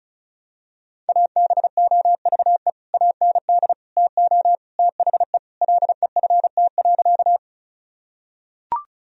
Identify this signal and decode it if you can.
Morse code, heard as a keyed tone, about 36 wpm, 700 Hz. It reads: ABOVE AND TO THE LEFT.
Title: Sentences from Top 1000 Words - Review Part 2 - 35wpm